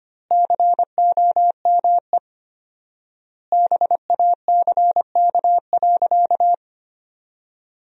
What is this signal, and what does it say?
Morse code, heard as a keyed tone, about 25 wpm, 700 Hz. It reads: COME BACK.
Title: 4x Word-Spacing - Sentences from Top 200 Words - 25wpm